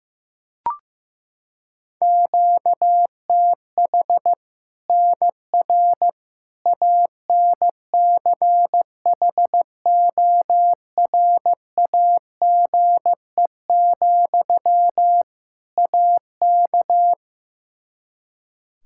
Morse code, heard as a keyed tone, about 16 wpm, 700 Hz. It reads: QTH NR ANCHORAGE, AK
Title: K1USN SST QSO Practice 15wpm